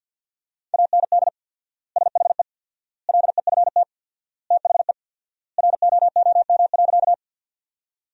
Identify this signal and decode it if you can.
Morse code, heard as a keyed tone, about 49 wpm, 700 Hz. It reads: AND SHE LEFT THE ROOM.
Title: 4x Word-Spacing - Sentences from Top 300 Words - 50wpm